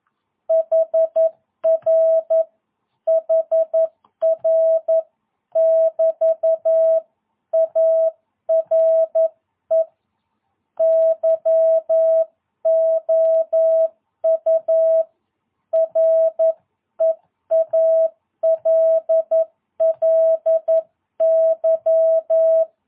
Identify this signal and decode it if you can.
Morse code, heard as a keyed tone, about 11 wpm, 660 Hz. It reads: HR HR <BT> ARE YOU REALLY
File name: jh8cbh / 2022cw